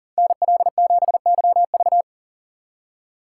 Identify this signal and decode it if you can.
Morse code, heard as a keyed tone, about 53 wpm, 700 Hz. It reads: NL7YV